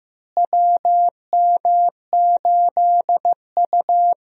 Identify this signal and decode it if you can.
Morse code, heard as a keyed tone, about 16 wpm, 700 Hz. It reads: WM8U
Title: USA Call Signs - 4 Characters - Repeated 3x - 15wpm